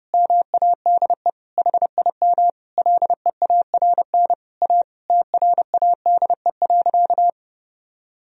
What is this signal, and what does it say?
Morse code, heard as a keyed tone, about 30 wpm, 700 Hz. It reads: MADE HIM LEARN A TRADE.
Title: Sentences from Top 1000 Words - Review Part 4 - 30wpm